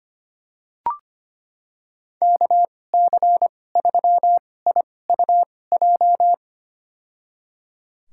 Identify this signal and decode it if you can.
Morse code, heard as a keyed tone, about 25 wpm, spaced as for 18 wpm, 700 Hz. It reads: KC3IUJ
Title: USA Call Signs - 6 Characters - 2x Character Spacing - 25wpm